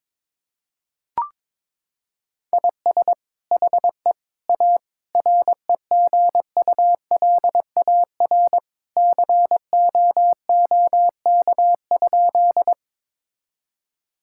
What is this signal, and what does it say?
Morse code, heard as a keyed tone, about 22 wpm, 700 Hz. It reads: IS HE A REGULAR COOK?
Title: Sentences from Top 1000 Words - Review Part 6 - 22wpm